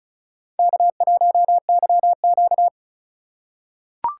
Morse code, heard as a keyed tone, about 34 wpm, 690 Hz. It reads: K1YQ
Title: USA Call Signs - 4 Characters - 2x Character Spacing - 35wpm